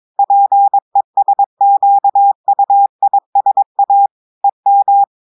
Morse code, heard as a keyed tone, about 22 wpm, 800 Hz. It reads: PESQUISA EM
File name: audios / portugais / portugais texte en clair 1 vitesse 22 mots minutes